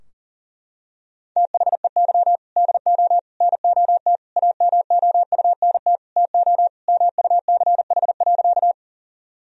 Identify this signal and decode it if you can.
Morse code, heard as a keyed tone, about 40 wpm, 700 Hz. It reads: THEY DO NOT AMOUNT TO MUCH.
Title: Sentences from Top 1000 Words - Review Part 4 - 40wpm